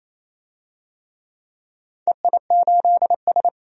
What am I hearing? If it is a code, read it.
Morse code, heard as a keyed tone, about 28 wpm, 700 Hz. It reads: EI8S
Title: DXCC-100 Call Signs - 4 Characters - 28wpm